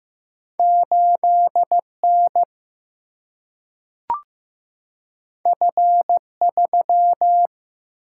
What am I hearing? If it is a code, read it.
Morse code, heard as a keyed tone, about 15 wpm, 700 Hz. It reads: 8 N F 3
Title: Single-Letter Single-Number 15wpm